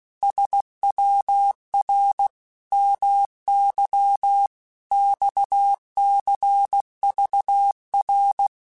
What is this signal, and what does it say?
Morse code, heard as a keyed tone, about 16 wpm, 800 Hz. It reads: SWR MY XCVR
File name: QSOs / QSOs-mp3s / WD0FBT 1@15